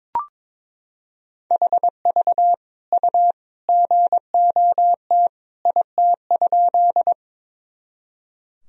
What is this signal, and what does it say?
Morse code, heard as a keyed tone, about 21 wpm, 700 Hz. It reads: HV U GOT IT?
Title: Sentences from Top 300 Words and Abbreviations 22wpm